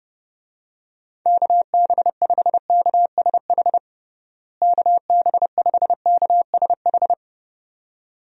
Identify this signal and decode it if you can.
Morse code, heard as a keyed tone, about 30 wpm, 700 Hz. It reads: KB5KSH KB5KSH
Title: USA Call Signs - 6 Characters - Repeated 2x - 30wpm